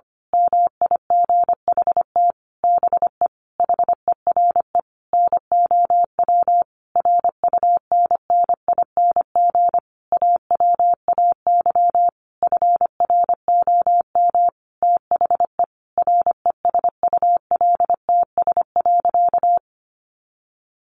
Morse code, heard as a keyed tone, about 25 wpm, 700 Hz. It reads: MIGHT BE HERE NOW RUNNING AWAY FROM THE RESULTS.
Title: A Sound Of Thunder 25wpm